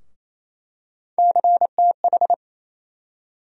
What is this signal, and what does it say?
Morse code, heard as a keyed tone, about 28 wpm, 700 Hz. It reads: CTH
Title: USA Call Sign Postfixes - 3 Letters 28wpm